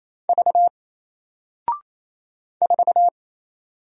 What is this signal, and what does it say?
Morse code, heard as a keyed tone, about 27 wpm, 700 Hz. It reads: V 4
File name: Single-Letter-Number-28wpm